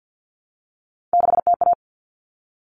Morse code, heard as a keyed tone, about 50 wpm, 700 Hz. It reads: 6TA
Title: All Cut Numbers - 3-Digit 50wpm